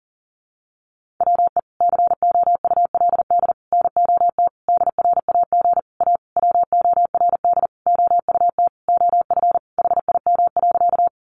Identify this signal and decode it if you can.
Morse code, heard as a keyed tone, about 40 wpm, 700 Hz. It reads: WE COULD NOT DRAG A WORD OUT OF HIM.
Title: Sentences from Top 1500 Words 40wpm